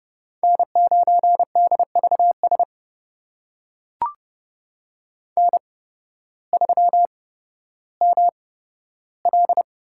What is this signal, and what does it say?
Morse code, heard as a keyed tone, about 31 wpm, spaced as for 14 wpm, 700 Hz. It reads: N9DVS N3ML
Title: USA Call Signs - 5 Characters - 8x Character Spacing - 30wpm